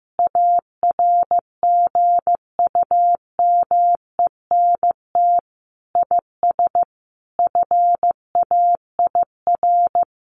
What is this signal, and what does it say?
Morse code, heard as a keyed tone, about 15 wpm, 700 Hz. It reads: ARGUMENT IS FAIR
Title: Sentences from Top 1400 Words 15wpm